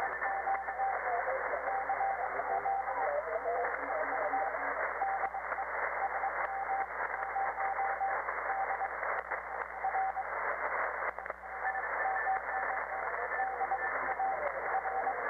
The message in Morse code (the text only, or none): none